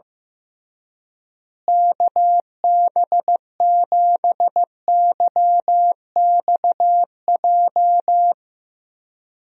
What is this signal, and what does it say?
Morse code, heard as a keyed tone, about 15 wpm, 700 Hz. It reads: KB7YXJ
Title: USA Call Signs - 6 Characters - Repeated 2x - 15wpm